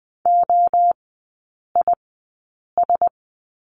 Morse code, heard as a keyed tone, about 20 wpm, spaced as for 8 wpm, 700 Hz. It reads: OIS
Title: 047 Lesson 9e - Number 4 - Three Characters 20wpm